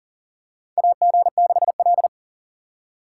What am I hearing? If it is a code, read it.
Morse code, heard as a keyed tone, about 38 wpm, 700 Hz. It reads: AG6L
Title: USA Call Signs - 4 Characters - 3x Character Spacing - 40wpm